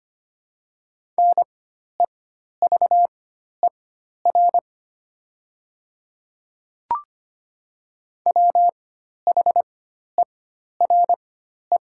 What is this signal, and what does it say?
Morse code, heard as a keyed tone, about 25 wpm, spaced as for 12 wpm, 700 Hz. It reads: NEVER WHERE
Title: 4x Character Spacing - Top 300 Words - 25wpm